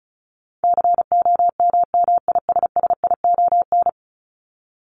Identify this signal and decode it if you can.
Morse code, heard as a keyed tone, about 34 wpm, 700 Hz. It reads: COMMISSION
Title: Top 1251-1500 Words 35wpm